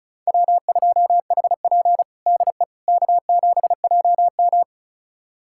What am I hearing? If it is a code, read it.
Morse code, heard as a keyed tone, about 34 wpm, 700 Hz. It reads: W2HP DE K7JM
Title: USA Callsign DE Callsign - 4 Characters - 35wpm